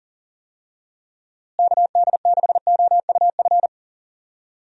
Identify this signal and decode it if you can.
Morse code, heard as a keyed tone, about 40 wpm, 700 Hz. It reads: KD6OUF
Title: USA Call Signs - 6 Characters - Repeated 3x - 40wpm